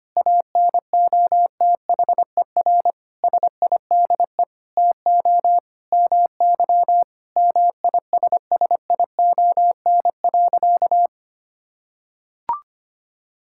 Morse code, heard as a keyed tone, about 25 wpm, 700 Hz. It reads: ANOTHER SIDE TO MY MISSION.
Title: Sentences from Top 1600 Words 25wpm